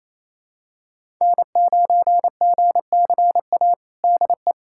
Morse code, heard as a keyed tone, about 28 wpm, 700 Hz. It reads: N9GCA DE
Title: USA Callsign DE Callsign - 5 Characters - 28wpm